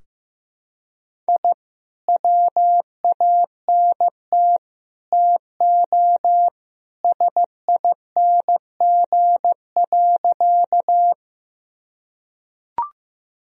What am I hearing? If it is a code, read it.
Morse code, heard as a keyed tone, about 15 wpm, 700 Hz. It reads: I WANT TO SING.
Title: Sentences from Top 1000 Words - Review Part 2 - 15wpm